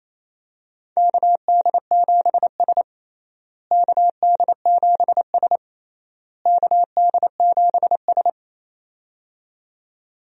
Morse code, heard as a keyed tone, about 28 wpm, 700 Hz. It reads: KD7S KD7S KD7S
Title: USA Call Signs - 4 Characters - Repeated 3x - 28wpm